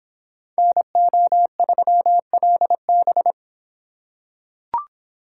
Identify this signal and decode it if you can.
Morse code, heard as a keyed tone, about 26 wpm, 700 Hz. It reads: NO3LB